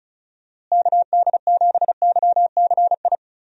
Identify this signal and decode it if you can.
Morse code, heard as a keyed tone, about 36 wpm, 700 Hz. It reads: KD7YCI
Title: USA Call Signs - 6 Characters - Repeated 2x - 35wpm